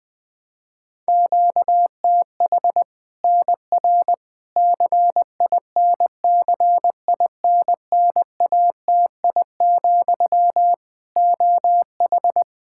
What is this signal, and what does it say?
Morse code, heard as a keyed tone, about 20 wpm, 700 Hz. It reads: QTH NR CINCINNATI, OH
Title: Practice QSO - Exchange 1 - Beginning - 20wpm